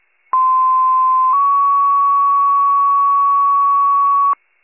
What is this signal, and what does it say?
Two-tone page, heard as a one-second tone, then a three-second tone, about 1040 then 1120 Hz. It is heard from 0.3 to 4.3 s.
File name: pager / 95 RESCUE / wcalert